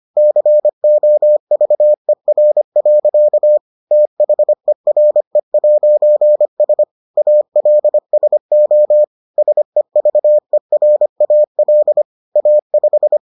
Morse code, heard as a keyed tone, about 25 wpm, 600 Hz. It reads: COVER. THERE'S ALSO SEVERAL A5